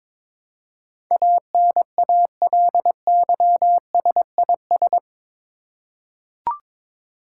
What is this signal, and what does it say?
Morse code, heard as a keyed tone, about 22 wpm, 700 Hz. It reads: ANALYSIS
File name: practice / 22wpm / Eight Letter Words - 22wpm